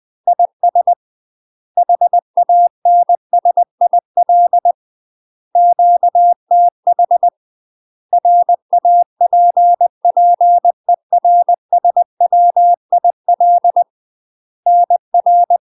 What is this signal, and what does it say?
Morse code, heard as a keyed tone, about 20 wpm, 700 Hz. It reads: IS HANSIL QTH RAPPERSWIL NR
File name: UK MASTER QSOs TRAINING / 20WPM / UK MASTER QSOs 0055